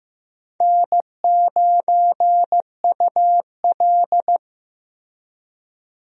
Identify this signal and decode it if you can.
Morse code, heard as a keyed tone, about 15 wpm, 700 Hz. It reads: N9UL
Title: USA Call Signs - 4 Characters - Repeated 3x - 15wpm